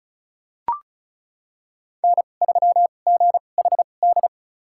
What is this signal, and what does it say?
Morse code, heard as a keyed tone, about 36 wpm, spaced as for 25 wpm, 700 Hz. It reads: N3GHD
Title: USA Call Signs - 5 Characters - 2x Character Spacing - 35wpm